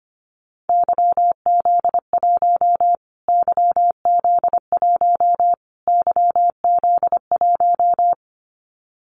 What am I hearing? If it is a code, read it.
Morse code, heard as a keyed tone, about 25 wpm, 700 Hz. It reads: YZ1 YZ1 YZ1
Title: Callsign Prefixes for Top 25 DXCC - Repeated 3x - 25wpm